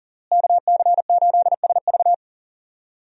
Morse code, heard as a keyed tone, about 40 wpm, 700 Hz. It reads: KC8SV
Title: USA Call Signs - 5 Characters - Repeated 3x - 40wpm